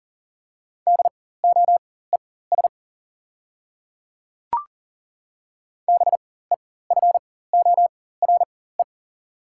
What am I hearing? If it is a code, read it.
Morse code, heard as a keyed tone, about 39 wpm, spaced as for 19 wpm, 690 Hz. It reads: DOES BEFORE
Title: 4x Character Spacing - Top 300 Words - 40wpm